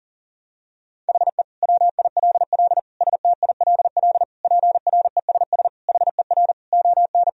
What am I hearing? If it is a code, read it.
Morse code, heard as a keyed tone, about 40 wpm, 700 Hz. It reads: HE WILL STILL PRESS HER ON
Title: Sentences from Top 1000 Words - Review Part 1 - 40wpm